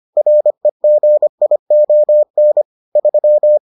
Morse code, heard as a keyed tone, about 25 wpm, 600 Hz. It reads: REGION 3